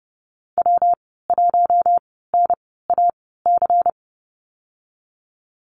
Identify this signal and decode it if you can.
Morse code, heard as a keyed tone, about 30 wpm, spaced as for 17 wpm, 700 Hz. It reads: W1NAC
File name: Lesson 19g - Letter w - Call Signs 30wpm